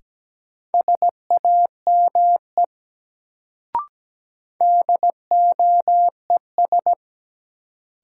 Morse code, heard as a keyed tone, about 17 wpm, 700 Hz. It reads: SAME DOES
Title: Sentences from Top 200 Words - New Word Drill - 17wpm